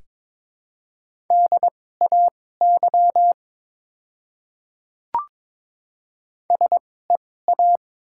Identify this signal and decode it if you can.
Morse code, heard as a keyed tone, about 22 wpm, spaced as for 16 wpm, 700 Hz. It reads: DAY SEA
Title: 2x Character Spacing - Top 500 Words - 22wpm